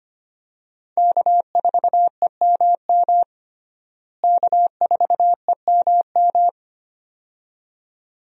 Morse code, heard as a keyed tone, about 25 wpm, 700 Hz. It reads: K4EMM K4EMM
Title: USA Call Signs - 5 Characters - Repeated 2x - 25wpm